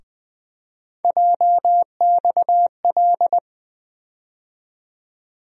Morse code, heard as a keyed tone, about 20 wpm, 700 Hz. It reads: JXL
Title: USA Call Sign Postfixes - 3 Letters 20wpm